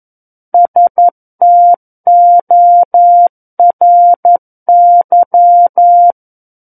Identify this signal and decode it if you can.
Morse code, heard as a keyed tone, about 11 wpm, 700 Hz. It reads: STORY
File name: 11 WPM